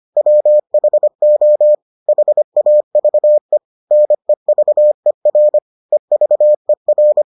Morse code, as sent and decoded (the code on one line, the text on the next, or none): .-- .... --- / .... .- ...- . / -. ...- . .-. / . ...- . .-.
WHO HAVE NEVER EVER